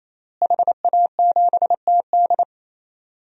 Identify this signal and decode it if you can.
Morse code, heard as a keyed tone, about 28 wpm, 700 Hz. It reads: HA7TD
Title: DXCC-100 Call Signs - 5 Characters - Repeated 3x - 28wpm